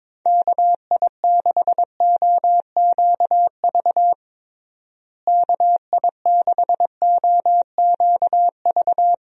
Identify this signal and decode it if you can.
Morse code, heard as a keyed tone, about 22 wpm, 700 Hz. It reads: KI6OQV KI6OQV
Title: USA Call Signs - 6 Characters - Repeated 2x - 22wpm